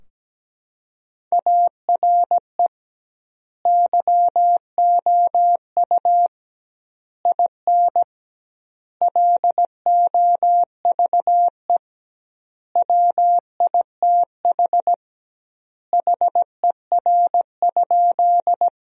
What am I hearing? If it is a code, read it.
Morse code, heard as a keyed tone, about 17 wpm, 700 Hz. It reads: ARE YOU IN LOVE WITH HER?
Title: 2x Word-Spacing - Sentences from Top 400 Words - 17wpm